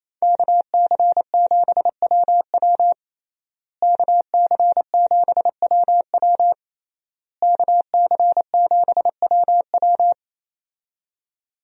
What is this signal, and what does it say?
Morse code, heard as a keyed tone, about 28 wpm, 700 Hz. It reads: KC7WW KC7WW KC7WW
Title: USA Call Signs - 5 Characters - Repeated 3x - 28wpm